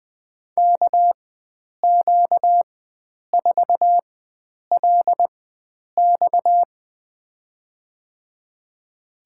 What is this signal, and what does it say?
Morse code, heard as a keyed tone, about 21 wpm, spaced as for 9 wpm, 700 Hz. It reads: KQ4LX
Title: USA Call Signs - 5 Characters - 4x Character Spacing - 20wpm